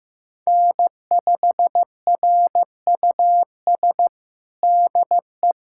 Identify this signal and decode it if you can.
Morse code, heard as a keyed tone, about 15 wpm, 700 Hz. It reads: N5RUS DE
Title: USA Callsign DE Callsign - 5 Characters - 15wpm